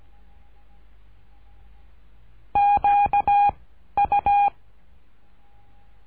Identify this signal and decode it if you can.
Morse code, heard as a keyed tone, about 17 wpm, 810 Hz. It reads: Q U
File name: cursos / CURSO DE TELEGRAFIA / Telegrafia - Trilha 13